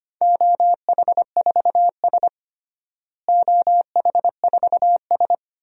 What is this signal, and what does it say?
Morse code, heard as a keyed tone, about 25 wpm, 700 Hz. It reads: OH4S OH4S
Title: DXCC-100 Call Signs - 4 Characters - Repeated 3x - 25wpm